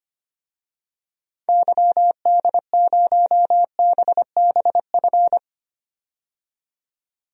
Morse code, heard as a keyed tone, about 25 wpm, 700 Hz. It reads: YD0BBF
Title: DXCC-100 Call Signs - 6 Characters - 25wpm